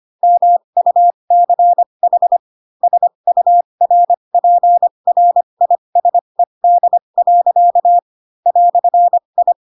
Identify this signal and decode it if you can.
Morse code, heard as a keyed tone, about 25 wpm, spaced as for 22 wpm, 700 Hz. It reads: MUCH SURPRISED. "I
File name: Pinocchio / Pinocchio ch7 18 25 ik1hge